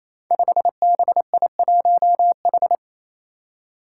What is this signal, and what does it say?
Morse code, heard as a keyed tone, about 27 wpm, 700 Hz. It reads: 5BI1H
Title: Five Challenging Letter-Number Combinations 28wpm